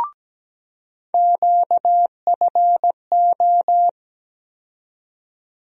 Morse code, heard as a keyed tone, about 17 wpm, 700 Hz. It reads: QFO
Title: USA Call Sign Postfixes - 3 Letters 17wpm